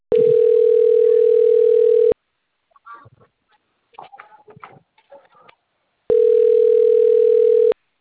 Ringback tone (North America), 440 + 480 Hz.